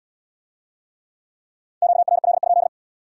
Morse code, heard as a keyed tone, about 67 wpm, 700 Hz. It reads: QRL?